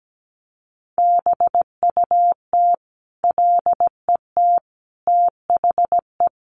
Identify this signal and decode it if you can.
Morse code, heard as a keyed tone, about 17 wpm, 700 Hz. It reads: BUT LET THE
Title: Sentences from Top 1300 Words 17wpm